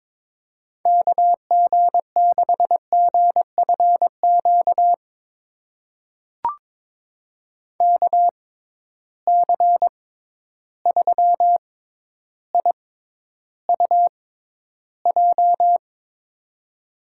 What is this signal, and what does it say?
Morse code, heard as a keyed tone, about 22 wpm, spaced as for 11 wpm, 700 Hz. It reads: KG6GFQ KC3IUJ
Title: USA Call Signs - 6 Characters - 6x Character Spacing - 22wpm